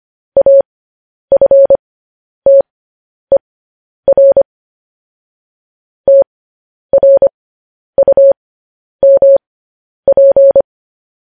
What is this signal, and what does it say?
Morse code, heard as a keyed tone, about 25 wpm, spaced as for 10 wpm, 550 Hz. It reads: AFTER TRUMP
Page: news headlines in morse code at 10 wpm